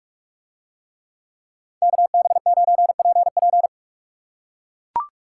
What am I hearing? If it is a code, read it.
Morse code, heard as a keyed tone, about 44 wpm, 700 Hz. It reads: KB9PP